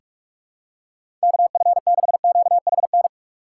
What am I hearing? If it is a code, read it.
Morse code, heard as a keyed tone, about 41 wpm, 700 Hz. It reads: KF6QHN
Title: USA Call Signs - 6 Characters - 6x Character Spacing - 45wpm